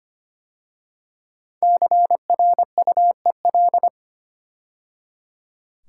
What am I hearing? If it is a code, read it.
Morse code, heard as a keyed tone, about 25 wpm, 700 Hz. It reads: CRUEL
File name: 5 Letter Words - First 3 Letters Given - 25wpm